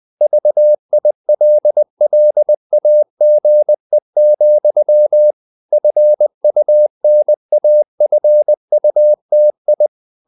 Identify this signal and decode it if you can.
Morse code, heard as a keyed tone, about 20 wpm, 600 Hz. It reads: VILLAGE, FUNAFUTI
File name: CW / Practice / NationalCapitals20